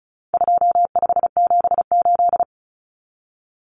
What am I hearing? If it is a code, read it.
Morse code, heard as a keyed tone, about 34 wpm, 700 Hz. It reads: 2578
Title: Common Cut Numbers - 4-Digit 35wpm